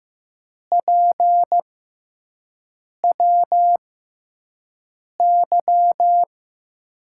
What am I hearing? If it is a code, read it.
Morse code, heard as a keyed tone, about 15 wpm, spaced as for 5 wpm, 700 Hz. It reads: PWY
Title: USA Call Signs - 5 Characters - 6x Character Spacing - 15wpm